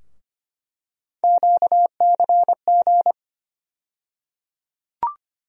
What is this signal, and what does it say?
Morse code, heard as a keyed tone, about 25 wpm, 700 Hz. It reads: QCG
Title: USA Call Sign Postfixes - 3 Letters 25wpm